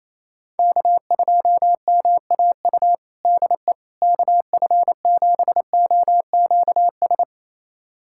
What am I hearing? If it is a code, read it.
Morse code, heard as a keyed tone, about 28 wpm, 700 Hz. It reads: K2MAU DE KF7OQS
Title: USA Callsign DE Callsign 28wpm